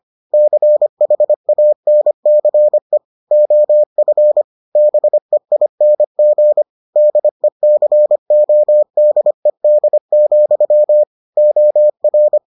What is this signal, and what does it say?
Morse code, heard as a keyed tone, about 25 wpm, 600 Hz. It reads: CHANCE OF BEING DECODED, OR